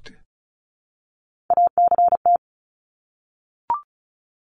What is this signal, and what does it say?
Morse code, heard as a keyed tone, about 34 wpm, 700 Hz. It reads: ACT
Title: Top 300 Words Repeated 35wpm